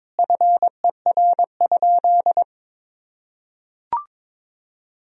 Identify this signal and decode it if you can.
Morse code, heard as a keyed tone, about 22 wpm, 700 Hz. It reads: FER?
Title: Sentences from Top 400 Words and Abbreviations 22wpm